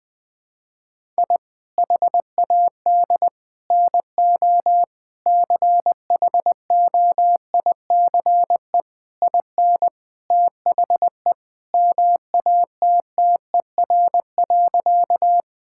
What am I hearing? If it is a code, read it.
Morse code, heard as a keyed tone, about 20 wpm, 700 Hz. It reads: I HAD NO CHOICE IN THE MATTER.
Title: Sentences from Top 1000 Words - Review Part 6 - 20wpm